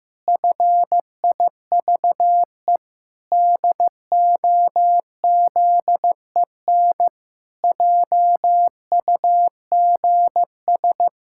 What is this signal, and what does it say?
Morse code, heard as a keyed tone, about 15 wpm, 700 Hz. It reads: FIVE DOZEN JUGS
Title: Pangram Challenge v2 - 15wpm